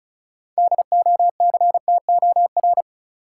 Morse code, heard as a keyed tone, about 34 wpm, 700 Hz. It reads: DOCTOR
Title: Sentences from Top 1500 Words - New Word Drill - 35wpm